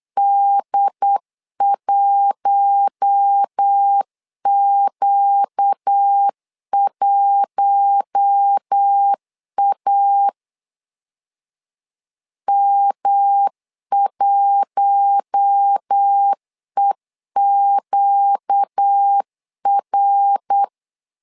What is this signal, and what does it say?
Morse code, heard as a keyed tone, about 8 wpm, 800 Hz. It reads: D1Q1A M1EQR
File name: LEZIONI / 4 Q1 8wpm 3035